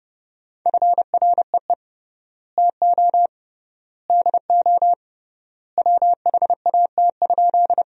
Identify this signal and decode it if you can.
Morse code, heard as a keyed tone, about 32 wpm, 700 Hz. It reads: FREE TO DO WHAT?
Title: 3x Word-Spacing - Sentences from Top 500 Words - 30wpm